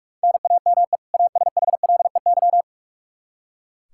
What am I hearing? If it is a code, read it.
Morse code, heard as a keyed tone, about 41 wpm, 700 Hz. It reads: NAME ASHLEY
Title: Practice QSO - Exchange 1 - Beginning - 45wpm